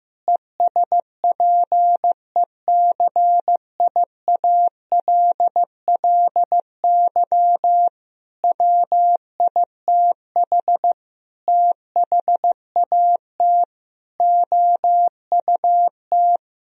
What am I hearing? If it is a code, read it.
Morse code, heard as a keyed tone, about 15 wpm, 700 Hz. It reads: ESPECIALLY WITH THAT OUT
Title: Sentences from Top 1000 Words - Review Part 4 - 15wpm